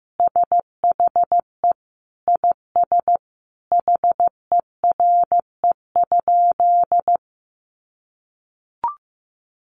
Morse code, heard as a keyed tone, about 15 wpm, 700 Hz. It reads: SHE IS HERE?
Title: Sentences from Top 1000 Words 15wpm